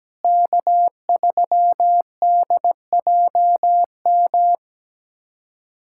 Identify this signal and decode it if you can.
Morse code, heard as a keyed tone, about 17 wpm, 700 Hz. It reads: K3DJM
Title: USA Callsign DE Callsign - 5 Characters - 17wpm